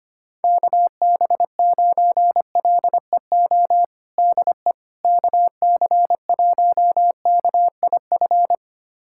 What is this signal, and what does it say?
Morse code, heard as a keyed tone, about 25 wpm, 700 Hz. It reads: KB9LEO DE KC1KIF
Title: USA Callsign DE Callsign - 6 Characters - 25wpm